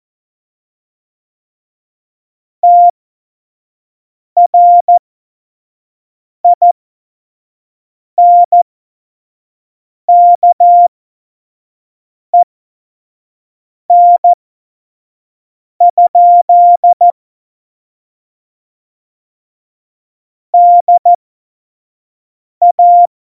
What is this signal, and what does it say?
Morse code, heard as a keyed tone, about 14 wpm, spaced as for 5 wpm, 700 Hz. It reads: TRINKEN? DA
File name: Morsekurs / Witze / Witz 01. Oktober 2019, 25 F70